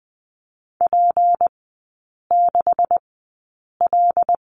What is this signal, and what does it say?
Morse code, heard as a keyed tone, about 20 wpm, spaced as for 8 wpm, 700 Hz. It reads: P6L
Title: 178 Lesson 29e - Slash - Three Characters 20wpm